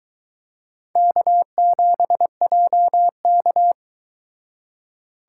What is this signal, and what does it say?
Morse code, heard as a keyed tone, about 23 wpm, 700 Hz. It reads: K7JK